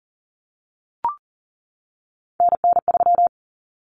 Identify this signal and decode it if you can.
Morse code, heard as a keyed tone, about 40 wpm, 700 Hz. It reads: NN3
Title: Common Cut Numbers - 3-Digit 40wpm